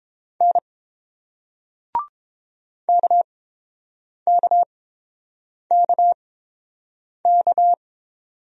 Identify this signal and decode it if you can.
Morse code, heard as a keyed tone, about 30 wpm, spaced as for 7 wpm, 700 Hz. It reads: N KKKK